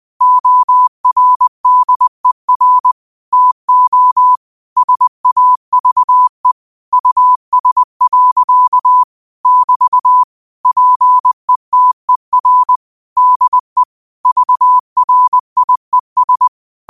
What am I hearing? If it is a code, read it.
Morse code, heard as a keyed tone, about 20 wpm, 1000 Hz. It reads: ORDER TO SAVE US. <BT> PETER DE VRIES
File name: qotdfiles / 20wpm